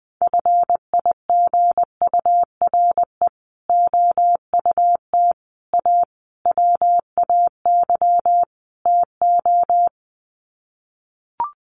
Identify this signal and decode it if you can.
Morse code, heard as a keyed tone, about 20 wpm, 700 Hz. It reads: FIGURE OUT A WAY TO
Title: Sets of 5 Words Encore2 20wpm